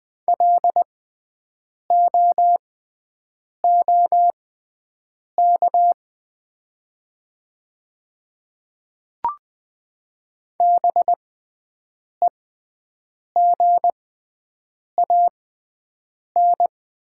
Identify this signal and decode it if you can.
Morse code, heard as a keyed tone, about 20 wpm, spaced as for 7 wpm, 700 Hz. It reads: LOOK BEGAN
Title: 6x Character Spacing - Top 300 Words - 20wpm